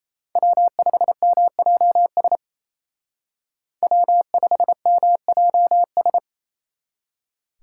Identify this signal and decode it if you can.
Morse code, heard as a keyed tone, about 35 wpm, 700 Hz. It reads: W5MJS W5MJS